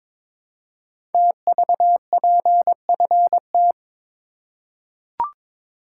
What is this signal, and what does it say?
Morse code, heard as a keyed tone, about 22 wpm, 700 Hz. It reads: TVPFT
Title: Code Groups - 5 Letters - 22wpm